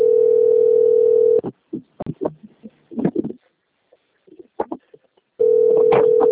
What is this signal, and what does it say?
Ringback tone (North America), 440 + 480 Hz.